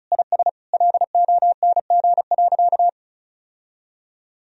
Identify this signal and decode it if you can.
Morse code, heard as a keyed tone, about 36 wpm, 700 Hz. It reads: IS LONG.
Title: Sentences from Top 1000 Words - Review Part 4 - 35wpm